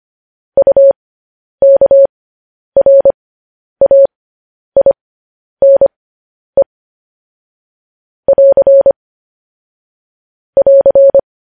Morse code, heard as a keyed tone, about 25 wpm, spaced as for 10 wpm, 550 Hz. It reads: UKRAINE <AR> <AR>
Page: News Headlines in Morse Code at 10 WPM